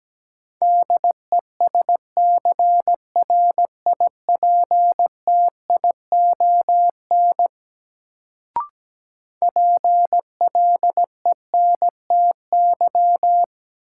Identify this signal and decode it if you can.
Morse code, heard as a keyed tone, about 17 wpm, 700 Hz. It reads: DESCRIPTION PLENTY
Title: Sentences from Top 1300 Words - New Word Drill - 17wpm